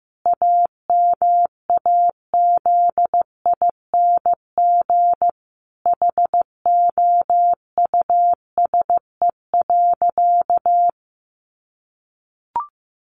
Morse code, heard as a keyed tone, about 15 wpm, 700 Hz. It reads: AMAZING HOUSE.
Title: Sentences from Top 1500 Words 15wpm